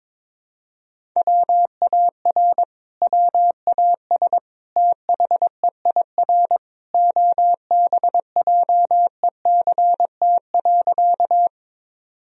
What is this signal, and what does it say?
Morse code, heard as a keyed tone, about 21 wpm, 700 Hz. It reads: WAR WAS THEIR OBJECT.
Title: Sentences from Top 1000 Words - Review Part 3 - 22wpm